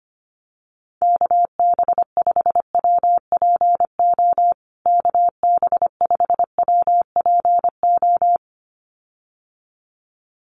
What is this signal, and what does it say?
Morse code, heard as a keyed tone, about 25 wpm, 700 Hz. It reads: KB5WPO KB5WPO
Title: USA Call Signs Repeated 2x 25wpm